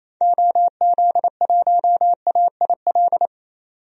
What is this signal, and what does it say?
Morse code, heard as a keyed tone, about 27 wpm, 700 Hz. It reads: OZ1AIL
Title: DXCC-100 Call Signs - 6 Characters - 28wpm